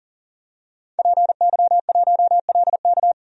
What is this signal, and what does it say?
Morse code, heard as a keyed tone, about 38 wpm, 700 Hz. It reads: PY1LK
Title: DXCC-100 Call Signs - 5 Characters - 40wpm